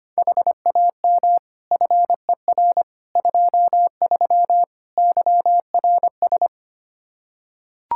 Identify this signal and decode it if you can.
Morse code, heard as a keyed tone, about 25 wpm, 700 Hz. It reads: HAM FER 23 YRS
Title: Practice QSO - Exchange 3 - More Info - 25wpm